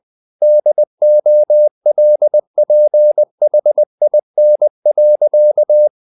Morse code, heard as a keyed tone, about 20 wpm, 600 Hz. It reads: DOLPHIN.
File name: the old man and the sea / the old man and the sea part 1 20wpm